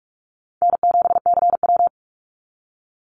Morse code, heard as a keyed tone, about 44 wpm, 700 Hz. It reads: N7CW
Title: NA SOTA Activator Callsigns 45wpm